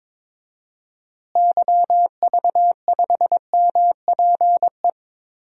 Morse code, heard as a keyed tone, about 21 wpm, 700 Hz. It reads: YV5MPE
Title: DXCC-100 Call Signs - 6 Characters - Repeated 3x - 22wpm